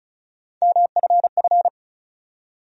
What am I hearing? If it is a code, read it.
Morse code, heard as a keyed tone, about 36 wpm, 700 Hz. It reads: MFF
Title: USA Call Sign Postfixes - 3 Letters 35wpm